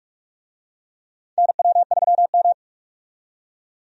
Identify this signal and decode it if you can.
Morse code, heard as a keyed tone, about 44 wpm, 700 Hz. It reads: NW3M
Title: USA Call Signs - 4 Characters - 45wpm